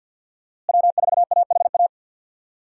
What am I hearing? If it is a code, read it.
Morse code, heard as a keyed tone, about 53 wpm, 700 Hz. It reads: W4AHA
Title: USA Call Signs - 5 Characters - Repeated 3x - 50wpm